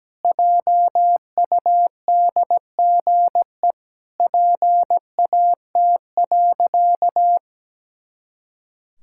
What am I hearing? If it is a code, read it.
Morse code, heard as a keyed tone, about 17 wpm, 700 Hz. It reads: JUDGE PAT.
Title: Pangram Challenge v2 - 17wpm